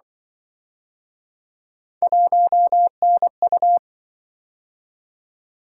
Morse code, heard as a keyed tone, about 24 wpm, 700 Hz. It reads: 1NU
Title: Speed-Racing - Three Letter-Number Combinations - 20wpm